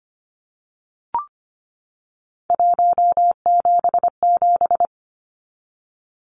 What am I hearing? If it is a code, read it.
Morse code, heard as a keyed tone, about 25 wpm, 700 Hz. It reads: 177
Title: Numbers - 3-Digit 25wpm